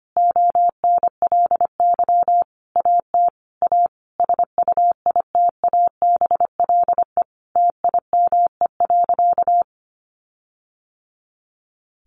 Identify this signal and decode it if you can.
Morse code, heard as a keyed tone, about 25 wpm, 700 Hz. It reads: ONLY AT A SUITABLE TIME.
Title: Sentences from Top 1400 Words 25wpm